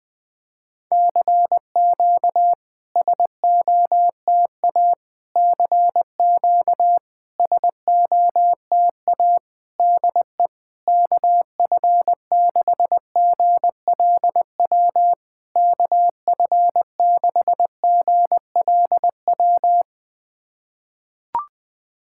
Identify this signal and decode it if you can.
Morse code, heard as a keyed tone, about 20 wpm, 700 Hz. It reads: CQ SOTA CQ SOTA DE KF6GLW KF6GLW
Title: SOTA QSO Practice 20wpm